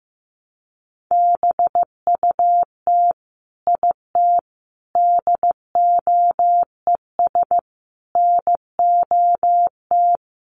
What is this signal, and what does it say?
Morse code, heard as a keyed tone, about 15 wpm, 700 Hz. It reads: BUT IT DOES NOT
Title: Sentences from Top 1200 Words 15wpm